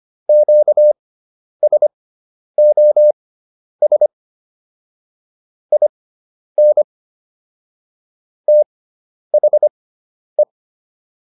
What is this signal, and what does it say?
Morse code, heard as a keyed tone, about 25 wpm, spaced as for 10 wpm, 600 Hz. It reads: QSOS IN THE